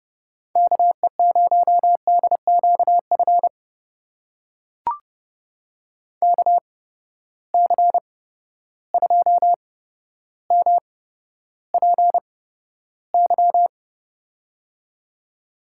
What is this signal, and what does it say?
Morse code, heard as a keyed tone, about 30 wpm, spaced as for 13 wpm, 700 Hz. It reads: KE0DQF KC2MPY
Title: USA Call Signs - 6 Characters - 8x Character Spacing - 30wpm